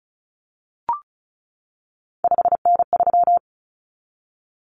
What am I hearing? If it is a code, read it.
Morse code, heard as a keyed tone, about 34 wpm, 700 Hz. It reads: 5N3
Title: Common Cut Numbers - 3-Digit 35wpm